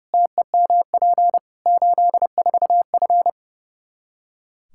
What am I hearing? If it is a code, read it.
Morse code, heard as a keyed tone, about 32 wpm, 700 Hz. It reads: TEMP 84F
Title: Practice QSO - Exchange 2 - Info - 30wpm